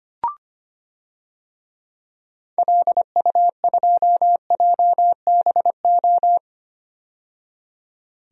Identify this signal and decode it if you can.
Morse code, heard as a keyed tone, about 25 wpm, 700 Hz. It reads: LU2JBO
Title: DXCC-100 Call Signs - 6 Characters - 25wpm